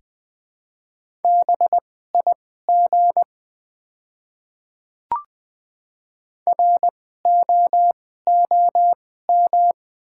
Morse code, heard as a keyed tone, about 20 wpm, spaced as for 14 wpm, 690 Hz. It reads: BIG ROOM
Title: 2x Character Spacing - Top 300 Words - 20wpm